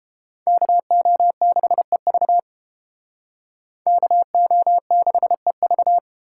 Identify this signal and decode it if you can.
Morse code, heard as a keyed tone, about 31 wpm, 700 Hz. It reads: KO6EV KO6EV